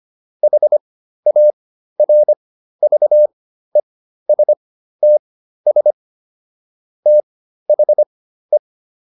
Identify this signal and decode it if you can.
Morse code, heard as a keyed tone, about 25 wpm, spaced as for 13 wpm, 600 Hz. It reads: HARVESTS THE